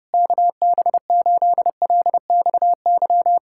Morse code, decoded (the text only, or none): KB8LXY